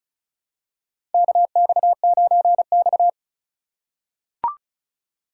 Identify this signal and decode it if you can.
Morse code, heard as a keyed tone, about 36 wpm, 700 Hz. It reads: KX9X
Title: USA Call Signs - 4 Characters - 6x Character Spacing - 35wpm